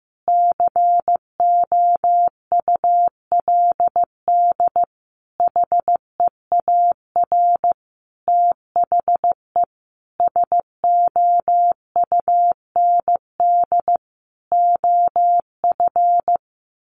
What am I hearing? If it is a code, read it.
Morse code, heard as a keyed tone, about 15 wpm, 700 Hz. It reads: COULD HEAR THE SOUND OF
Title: Sentences from Top 1100 Words 15wpm